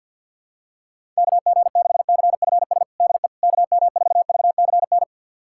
Morse code, heard as a keyed tone, about 52 wpm, 700 Hz. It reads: KG6CFS DE KM4VCN
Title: USA Callsign DE Callsign - 6 Characters - 50wpm